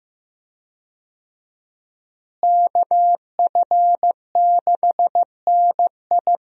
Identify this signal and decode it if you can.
Morse code, heard as a keyed tone, about 15 wpm, 700 Hz. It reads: KF6NI
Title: USA Call Signs - 5 Characters - Repeated 3x - 15wpm